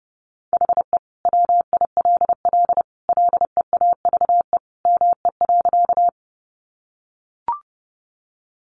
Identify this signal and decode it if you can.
Morse code, heard as a keyed tone, about 30 wpm, 700 Hz. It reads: HE WILL LEAVE ME.
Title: Sentences from Top 400 Words 30wpm